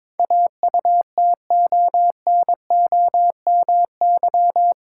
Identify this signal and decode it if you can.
Morse code, heard as a keyed tone, about 22 wpm, 700 Hz. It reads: AUTONOMY
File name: practice / 22wpm / Eight Letter Words - 22wpm